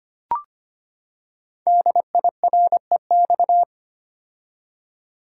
Morse code, heard as a keyed tone, about 25 wpm, 700 Hz. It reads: DIREX
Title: Code Groups - 5 Letters - 25wpm